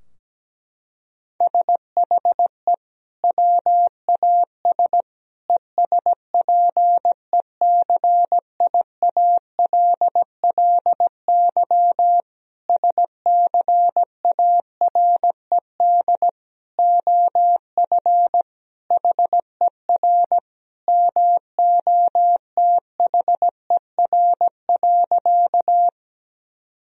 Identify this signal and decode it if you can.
Morse code, heard as a keyed tone, about 17 wpm, 700 Hz. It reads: SHE WAS ESPECIALLY SCARED OF HER MOTHER.
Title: Sentences from Top 1000 Words - Review Part 4 - 17wpm